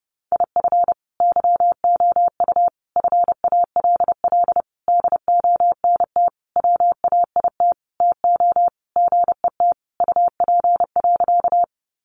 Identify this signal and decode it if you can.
Morse code, heard as a keyed tone, about 30 wpm, 700 Hz. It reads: IF YOU FALL DONT WAIT TO GET UP.
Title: Sentences from Top 500 Words 30wpm